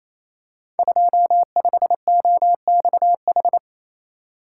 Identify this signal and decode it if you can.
Morse code, heard as a keyed tone, about 28 wpm, 700 Hz. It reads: 25OXH
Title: Code Groups - 5 Letter:Numbers - 28wpm